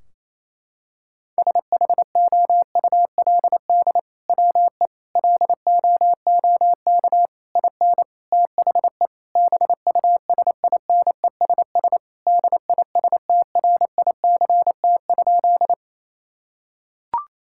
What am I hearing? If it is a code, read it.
Morse code, heard as a keyed tone, about 28 wpm, 700 Hz. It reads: SHOULD WE LOOK IN THE BUSINESS DISTRICT?
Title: Sentences from Top 1600 Words 28wpm